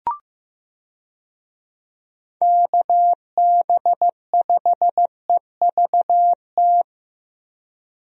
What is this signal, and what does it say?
Morse code, heard as a keyed tone, about 15 wpm, 700 Hz. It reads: KB5EVT
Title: USA Call Signs - 6 Characters - Repeated 3x - 15wpm